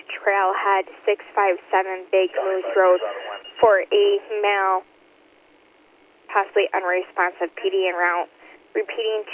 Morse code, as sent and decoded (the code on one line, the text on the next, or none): none